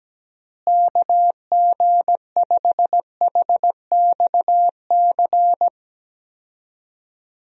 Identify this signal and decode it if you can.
Morse code, heard as a keyed tone, about 17 wpm, 700 Hz. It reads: KG5HXC
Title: USA Call Signs - 6 Characters - Repeated 3x - 17wpm